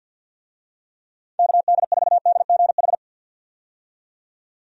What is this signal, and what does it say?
Morse code, heard as a keyed tone, about 51 wpm, 700 Hz. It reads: KD4DGH
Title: USA Call Signs - 6 Characters - 50wpm